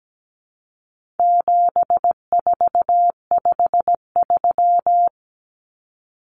Morse code, heard as a keyed tone, about 17 wpm, 700 Hz. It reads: 7453
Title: Numbers - 4-Digit 17wpm